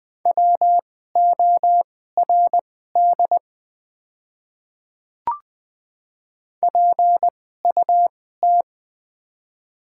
Morse code, heard as a keyed tone, about 20 wpm, spaced as for 14 wpm, 700 Hz. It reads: WORD PUT
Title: 2x Character Spacing - Top 500 Words - 20wpm